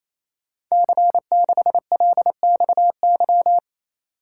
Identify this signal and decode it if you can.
Morse code, heard as a keyed tone, about 27 wpm, 700 Hz. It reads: C6LXY
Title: Code Groups - 5 Letter:Numbers - 28wpm